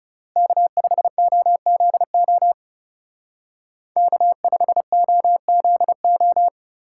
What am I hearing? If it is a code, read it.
Morse code, heard as a keyed tone, about 37 wpm, 700 Hz. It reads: K5OZO K5OZO
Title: Speed Racing - USA 1x3 Call Signs 30wpm